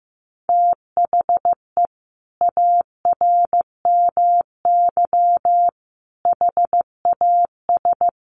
Morse code, heard as a keyed tone, about 15 wpm, 700 Hz. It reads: THE ARMY HAS
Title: Sentences from Top 1300 Words 15wpm